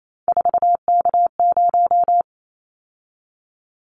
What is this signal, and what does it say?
Morse code, heard as a keyed tone, about 28 wpm, 700 Hz. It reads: 4K0
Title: Callsign Prefixes for Top 100 DXCC - 28wpm